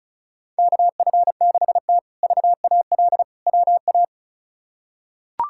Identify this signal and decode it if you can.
Morse code, heard as a keyed tone, about 36 wpm, 700 Hz. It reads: KF6T VAL WA
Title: K1USN SST QSO Practice - 35wpm